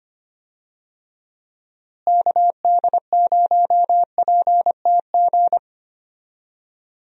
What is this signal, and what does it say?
Morse code, heard as a keyed tone, about 25 wpm, 700 Hz. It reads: KD0PTG
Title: USA Call Signs - 6 Characters - Repeated 3x - 25wpm